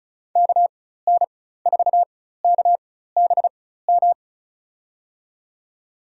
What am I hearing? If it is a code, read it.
Morse code, heard as a keyed tone, about 36 wpm, spaced as for 16 wpm, 700 Hz. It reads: KN4KBM
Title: USA Call Signs - 4x Character Spacing - 35wpm